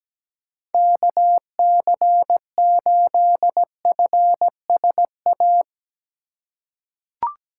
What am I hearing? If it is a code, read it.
Morse code, heard as a keyed tone, about 17 wpm, 700 Hz. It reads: KC8FSA
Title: USA Call Signs - 6 Characters - 4x Character Spacing - 17wpm